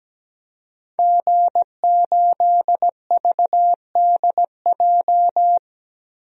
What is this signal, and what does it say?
Morse code, heard as a keyed tone, about 17 wpm, 700 Hz. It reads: G8VDJ
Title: DXCC-100 Call Signs - 5 Characters - Repeated 2x - 17wpm